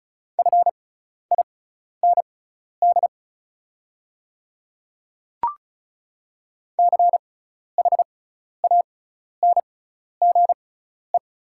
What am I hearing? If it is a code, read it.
Morse code, heard as a keyed tone, about 36 wpm, spaced as for 12 wpm, 690 Hz. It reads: FIND CHANGE
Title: 6x Character Spacing - Top 200 Words - 35wpm